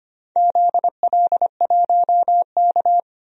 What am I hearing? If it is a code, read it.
Morse code, heard as a keyed tone, about 25 wpm, 700 Hz. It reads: ZL1K
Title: DXCC-100 Call Signs - 4 Characters - 25wpm